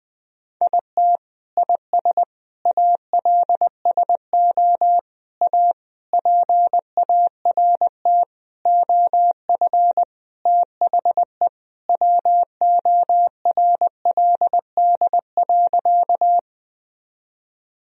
Morse code, heard as a keyed tone, about 20 wpm, 700 Hz. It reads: IT IS ALSO A PART OF THE WORLD.
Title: Sentences from Top 1000 Words - Review Part 1 - 20wpm